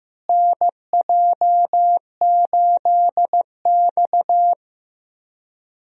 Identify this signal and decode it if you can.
Morse code, heard as a keyed tone, about 15 wpm, 700 Hz. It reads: NJ8X
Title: USA Call Signs - 4 Characters - Repeated 2x - 15wpm